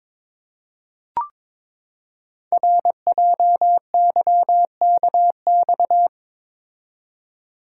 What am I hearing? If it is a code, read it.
Morse code, heard as a keyed tone, about 22 wpm, 700 Hz. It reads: RJYKX
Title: Code Groups - 5 Letters - 22wpm